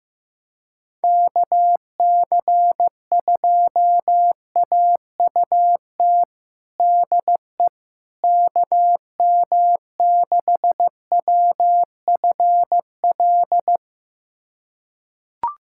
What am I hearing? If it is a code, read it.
Morse code, heard as a keyed tone, about 15 wpm, 700 Hz. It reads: KC2AUT DE KM6WFL
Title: USA Callsign DE Callsign - 6 Characters - 15wpm